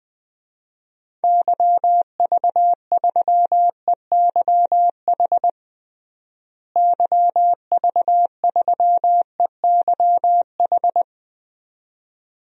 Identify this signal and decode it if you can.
Morse code, heard as a keyed tone, about 21 wpm, 700 Hz. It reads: YV3EYH YV3EYH
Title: DXCC-100 Call Signs - 6 Characters - Repeated 2x - 20wpm